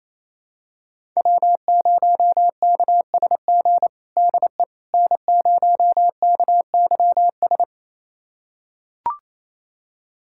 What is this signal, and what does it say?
Morse code, heard as a keyed tone, about 28 wpm, 700 Hz. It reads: W0KSG DE N0KYS
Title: USA Callsign DE Callsign - 5 Characters - 28wpm